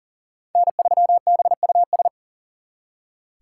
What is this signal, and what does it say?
Morse code, heard as a keyed tone, about 52 wpm, 700 Hz. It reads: N3BUS